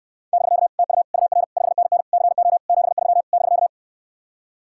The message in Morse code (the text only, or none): <BK> TU UR 5NN CO 73 <BK>